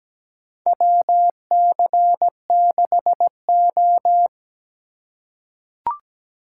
Morse code, heard as a keyed tone, about 17 wpm, 700 Hz. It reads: WC6O
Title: USA Call Signs - 4 Characters - 4x Character Spacing - 17wpm